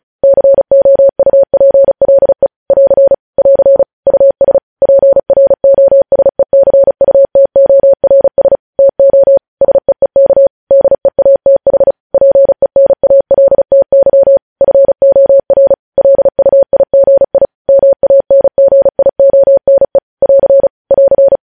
Morse code, heard as a keyed tone, about 36 wpm, 550 Hz. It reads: COUPLE <AR> <AR> US PROSECUTORS TO SEEK DEATH PENALTY FOR LUIGI MANGIONE <AR> <AR>
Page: news headlines in morse code at 35 wpm